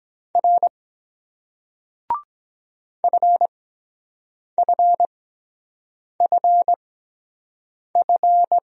Morse code, heard as a keyed tone, about 23 wpm, spaced as for 7 wpm, 700 Hz. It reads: R FFFF